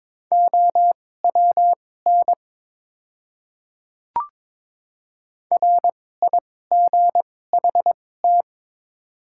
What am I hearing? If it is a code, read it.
Morse code, heard as a keyed tone, about 21 wpm, spaced as for 16 wpm, 700 Hz. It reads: OWN RIGHT
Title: 2x Character Spacing - Top 300 Words - 22wpm